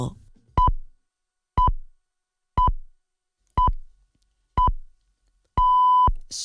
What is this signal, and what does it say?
Radio time signal (time pips), with speech. It is heard from 0.6 to 6.1 s.